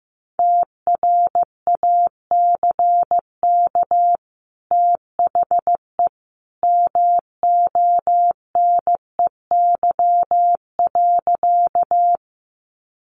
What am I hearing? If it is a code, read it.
Morse code, heard as a keyed tone, about 15 wpm, 700 Hz. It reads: TRACK THE MONEY.
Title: Sentences from Top 1100 Words 15wpm